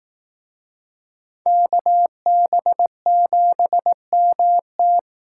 Morse code, heard as a keyed tone, about 18 wpm, 700 Hz. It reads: KB7MT
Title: Speed Racing - USA Call Signs 15wpm